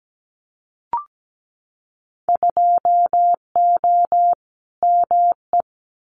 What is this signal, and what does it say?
Morse code, heard as a keyed tone, about 17 wpm, 690 Hz. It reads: 2O ME
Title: Winter Field Day Exchanges 17wpm